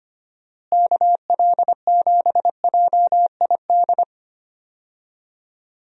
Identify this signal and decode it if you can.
Morse code, heard as a keyed tone, about 25 wpm, 700 Hz. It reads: KL7JID
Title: DXCC-100 Call Signs - 6 Characters - Repeated 2x - 25wpm